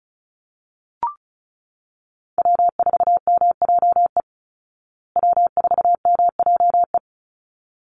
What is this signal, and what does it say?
Morse code, heard as a keyed tone, about 34 wpm, 700 Hz. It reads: W4MJE W4MJE
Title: Call Signs Repeated 3x - 35wpm